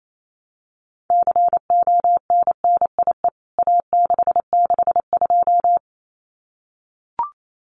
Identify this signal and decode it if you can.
Morse code, heard as a keyed tone, about 28 wpm, 700 Hz. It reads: CONNIE A662
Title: CWT Exchange 28wpm